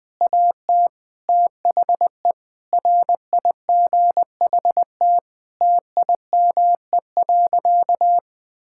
Morse code, heard as a keyed tone, about 21 wpm, 700 Hz. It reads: AT THE RIGHT TIME.
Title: Sentences with 6-Word Phrases - 20wpm